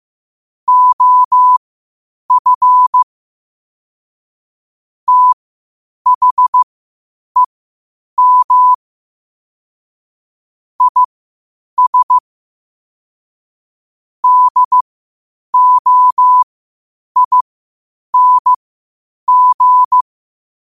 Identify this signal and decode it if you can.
Morse code, heard as a keyed tone, about 15 wpm, spaced as for 8 wpm, 1000 Hz. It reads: OF THEM IS DOING
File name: qotdfiles / 8wpm